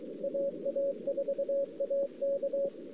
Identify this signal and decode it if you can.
Morse code, heard as a keyed tone, about 23 wpm, 560 Hz. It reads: AA4AK
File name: CW / 2008-ss-cw-k5tr-op-n5rz -03:00